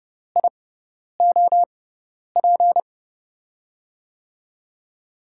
Morse code, heard as a keyed tone, about 30 wpm, spaced as for 10 wpm, 700 Hz. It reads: IOP
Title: USA Call Signs - 6 Characters - 6x Character Spacing - 30wpm